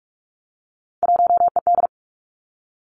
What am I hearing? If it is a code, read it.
Morse code, heard as a keyed tone, about 46 wpm, 700 Hz. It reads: 1ED
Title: Three Challenging Letter-Number Combinations 45wpm